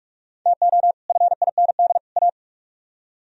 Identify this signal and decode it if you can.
Morse code, heard as a keyed tone, about 42 wpm, 700 Hz. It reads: TO FIND A